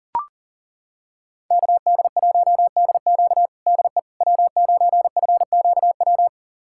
Morse code, heard as a keyed tone, about 40 wpm, 700 Hz. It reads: KD1DQ DE W9FQW
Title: USA Callsign DE Callsign - 5 Characters - 40wpm